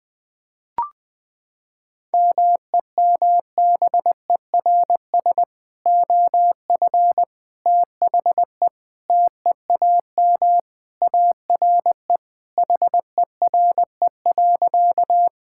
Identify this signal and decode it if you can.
Morse code, heard as a keyed tone, about 21 wpm, 700 Hz. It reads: MEMBERS OF THE TEAM ARE HERE.
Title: Sentences with 3-Word Phrases - 20wpm